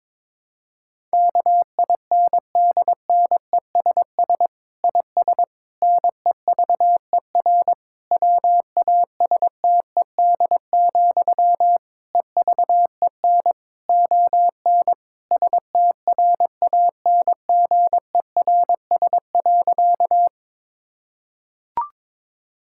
Morse code, heard as a keyed tone, about 22 wpm, 700 Hz. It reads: KINDNESS IS NEVER WASTED, EVEN ON STRANGERS.
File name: Difficult Sentences with Hints - 22wpm